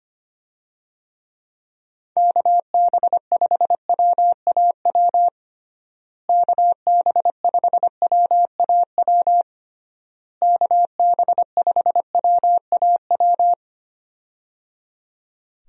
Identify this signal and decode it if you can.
Morse code, heard as a keyed tone, about 25 wpm, 700 Hz. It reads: KB5WAW KB5WAW KB5WAW
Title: USA Call Signs - 6 Characters - Repeated 3x - 25wpm